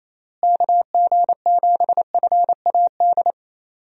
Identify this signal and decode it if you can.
Morse code, heard as a keyed tone, about 28 wpm, 700 Hz. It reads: KG7FAD
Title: USA Call Signs - 6 Characters - Repeated 2x - 28wpm